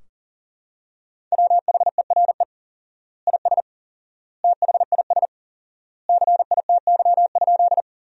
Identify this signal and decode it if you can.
Morse code, heard as a keyed tone, about 40 wpm, 700 Hz. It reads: WHERE IS THIS CITY?
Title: 4x Word-Spacing - Sentences from Top 300 Words - 40wpm